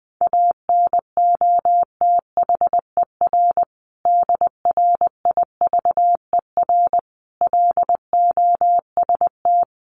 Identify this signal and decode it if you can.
Morse code, heard as a keyed tone, about 19 wpm, 700 Hz. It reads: ANOTHER DRIVER LOST
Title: Sentences from Top 1400 Words 20wpm